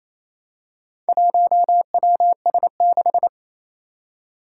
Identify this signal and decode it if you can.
Morse code, heard as a keyed tone, about 27 wpm, 700 Hz. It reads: 1WS6
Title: Four Challenging Letter-Number Combinations 28wpm